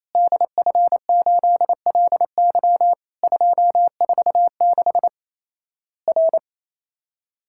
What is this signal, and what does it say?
Morse code, heard as a keyed tone, about 28 wpm, 700 Hz. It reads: DF8LY 246 R
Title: Worked All Europe DX Contest - QTC Exchange - 28wpm